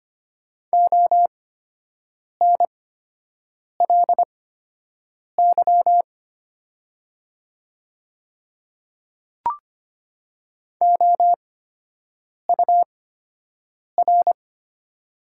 Morse code, heard as a keyed tone, about 25 wpm, spaced as for 7 wpm, 700 Hz. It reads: ONLY OUR
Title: 8x Character Spacing - Top 300 Words - 25wpm